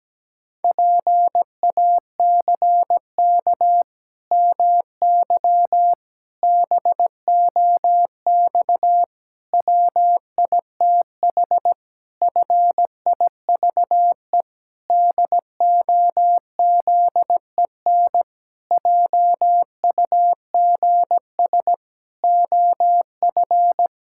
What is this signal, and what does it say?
Morse code, heard as a keyed tone, about 17 wpm, 700 Hz. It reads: PACK MY BOX WITH FIVE DOZEN JUGS OF
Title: Pangram Challenge v2 - 17wpm